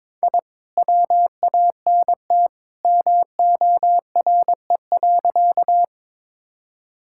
Morse code, heard as a keyed tone, about 22 wpm, 700 Hz. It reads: I WANT MORE.
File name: practice / 22wpm / Sentences with 2-Word Phrases - 22wpm